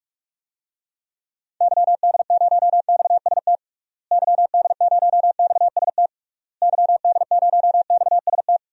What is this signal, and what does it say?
Morse code, heard as a keyed tone, about 46 wpm, 700 Hz. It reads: YD0XST YD0XST YD0XST
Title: DXCC-100 Call Signs - 6 Characters - Repeated 3x - 45wpm